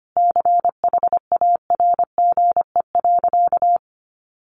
Morse code, heard as a keyed tone, about 25 wpm, 700 Hz. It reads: CHARGE.
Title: Sentences from Top 1300 Words 25wpm